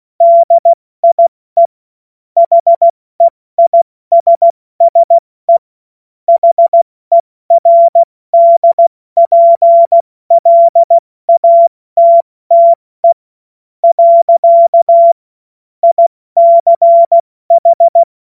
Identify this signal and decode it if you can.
Morse code, heard as a keyed tone, about 16 wpm, 680 Hz. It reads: DIE HEISSE HERDPLATTE . ICH
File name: Morsekurs / Witze / Witz 02 April 2024 70 F80